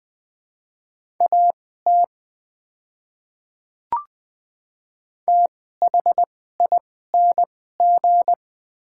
Morse code, heard as a keyed tone, about 20 wpm, spaced as for 14 wpm, 700 Hz. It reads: AT THING
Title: 2x Character Spacing - Top 100 Words - 20wpm